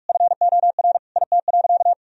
Morse code, heard as a keyed tone, about 41 wpm, 700 Hz. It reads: FOR IT.
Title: Sentences from Top 1000 Words - Review Part 3 - 45wpm